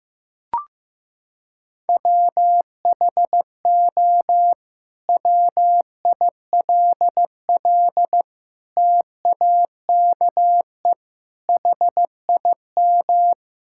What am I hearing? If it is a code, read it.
Morse code, heard as a keyed tone, about 16 wpm, 700 Hz. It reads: WHO WILL TAKE HIM
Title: Sentences from Top 1600 Words 15wpm